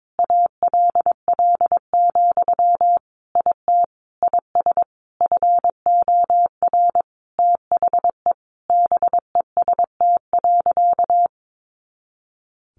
Morse code, heard as a keyed tone, about 22 wpm, 700 Hz. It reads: ALL, IT IS FOR THE BEST.
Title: Sentences from Top 500 Words 22wpm